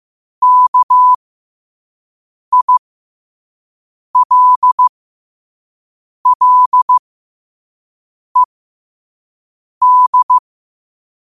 Morse code, heard as a keyed tone, about 15 wpm, spaced as for 5 wpm, 1000 Hz. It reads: KILLED
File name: 5wpm